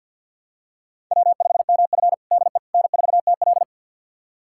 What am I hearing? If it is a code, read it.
Morse code, heard as a keyed tone, about 50 wpm, 700 Hz. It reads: W5MF DE N4TL
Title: USA Callsign DE Callsign - 4 Characters - 50wpm